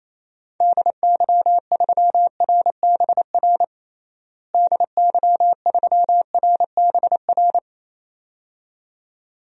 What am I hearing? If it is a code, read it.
Morse code, heard as a keyed tone, about 28 wpm, 700 Hz. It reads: DY3RBR DY3RBR
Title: DXCC-100 Call Signs - 6 Characters - Repeated 2x - 28wpm